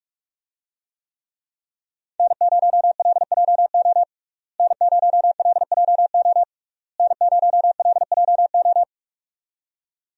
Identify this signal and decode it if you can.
Morse code, heard as a keyed tone, about 48 wpm, 700 Hz. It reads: N0LJO N0LJO N0LJO
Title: USA Call Signs - 5 Characters - Repeated 3x - 45wpm